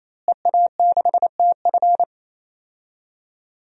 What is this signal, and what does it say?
Morse code, heard as a keyed tone, about 28 wpm, 700 Hz. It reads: EA6TF
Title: DXCC-100 Call Signs - 5 Characters - Repeated 3x - 28wpm